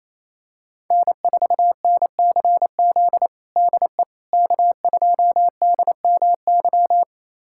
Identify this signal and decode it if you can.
Morse code, heard as a keyed tone, about 27 wpm, 700 Hz. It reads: N4NCZ DE K2DMY
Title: USA Callsign DE Callsign - 5 Characters - 28wpm